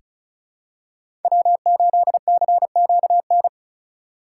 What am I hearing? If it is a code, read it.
Morse code, heard as a keyed tone, about 36 wpm, 690 Hz. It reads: W8CQN
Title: USA Call Signs - 5 Characters - 2x Character Spacing - 35wpm